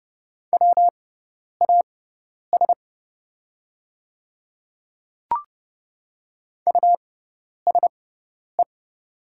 Morse code, heard as a keyed tone, about 30 wpm, spaced as for 10 wpm, 700 Hz. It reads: WAS USE
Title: 6x Character Spacing - Top 100 Words - 30wpm